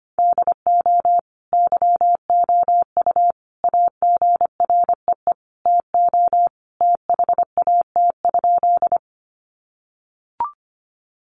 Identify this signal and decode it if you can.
Morse code, heard as a keyed tone, about 25 wpm, 700 Hz. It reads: DO YOU AGREE TO THAT?
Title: Sentences from Top 1100 Words 25wpm